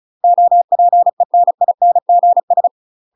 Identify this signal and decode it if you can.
Morse code, heard as a keyed tone, about 34 wpm, 700 Hz. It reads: OPENINGS
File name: UK MASTER QSOs TRAINING/35WPM